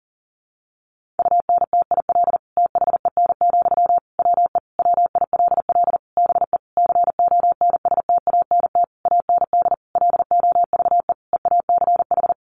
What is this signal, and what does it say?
Morse code, heard as a keyed tone, about 40 wpm, 700 Hz. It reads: UNTIL THEN, WE WILL BE CONSTANT AND LOVE EACH
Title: Sentences from Top 1500 Words 40wpm